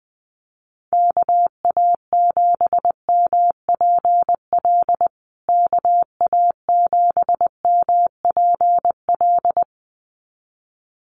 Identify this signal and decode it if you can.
Morse code, heard as a keyed tone, about 20 wpm, 700 Hz. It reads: KA7MPL KA7MPL
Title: USA Call Signs Repeated 2x 20wpm